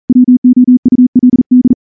Morse code, heard as a keyed tone, about 38 wpm, 250 Hz. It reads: WOULD
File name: morse-training-40WPM